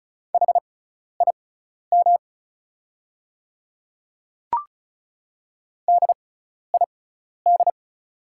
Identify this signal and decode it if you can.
Morse code, heard as a keyed tone, about 36 wpm, spaced as for 12 wpm, 700 Hz. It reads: HIM DID
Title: 6x Character Spacing - Top 100 Words - 35wpm